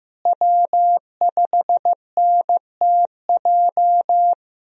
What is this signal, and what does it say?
Morse code, heard as a keyed tone, about 15 wpm, 700 Hz. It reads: W5NTJ
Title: USA Call Signs - 5 Characters - Repeated 3x - 15wpm